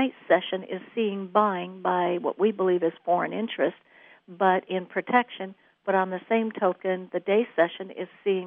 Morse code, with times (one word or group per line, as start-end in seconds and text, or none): none